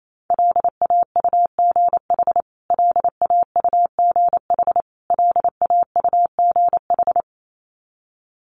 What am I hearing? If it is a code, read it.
Morse code, heard as a keyed tone, about 28 wpm, 700 Hz. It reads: LAUGH LAUGH LAUGH
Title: Top 500 Words Repeated 28wpm